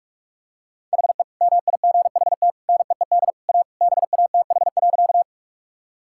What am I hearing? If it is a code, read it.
Morse code, heard as a keyed tone, about 42 wpm, 700 Hz. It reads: HE MIGHT NEED A BATH.
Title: Sentences from Top 1600 Words 45wpm